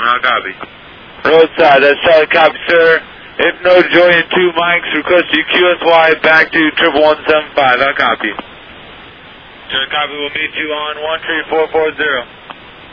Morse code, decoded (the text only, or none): none